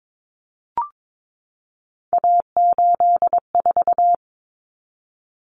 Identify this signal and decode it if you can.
Morse code, heard as a keyed tone, about 22 wpm, 700 Hz. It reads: A84
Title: Common Cut Numbers - 3-Digit 22wpm